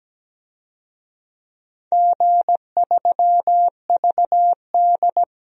Morse code, heard as a keyed tone, about 17 wpm, 700 Hz. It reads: G3VD
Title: DXCC-100 Call Signs - 4 Characters - 17wpm